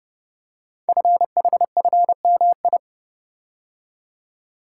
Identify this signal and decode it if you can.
Morse code, heard as a keyed tone, about 30 wpm, 700 Hz. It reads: FHFMI
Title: Code Groups - 5 Letter:Numbers - 30wpm